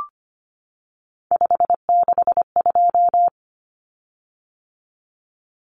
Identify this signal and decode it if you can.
Morse code, heard as a keyed tone, about 25 wpm, 700 Hz. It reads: 562
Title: Common Cut Numbers - 3-Digit 25wpm